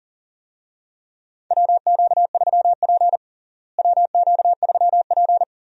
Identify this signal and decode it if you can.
Morse code, heard as a keyed tone, about 40 wpm, 700 Hz. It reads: WQ3P WQ3P
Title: USA Call Signs - 4 Characters - Repeated 3x - 40wpm